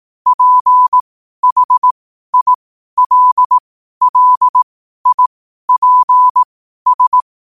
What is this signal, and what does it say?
Morse code, heard as a keyed tone, about 18 wpm, spaced as for 12 wpm, 1000 Hz. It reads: PHILLIPS
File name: qotdfiles / 12wpm